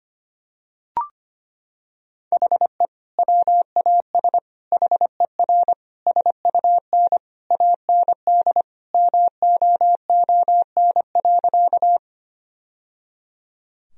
Morse code, heard as a keyed tone, about 25 wpm, 700 Hz. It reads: HE WAS HER SUN AND MOON.
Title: Sentences from Top 1000 Words - Review Part 3 - 25wpm